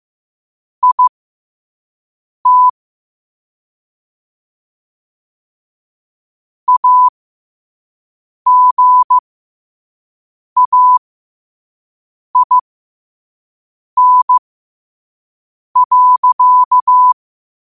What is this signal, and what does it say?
Morse code, heard as a keyed tone, about 15 wpm, spaced as for 5 wpm, 1000 Hz. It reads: IT AGAIN.
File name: qotdfiles / 5wpm